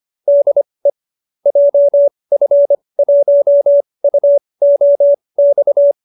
Morse code, heard as a keyed tone, about 25 wpm, spaced as for 20 wpm, 590 Hz. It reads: DE JF1UOX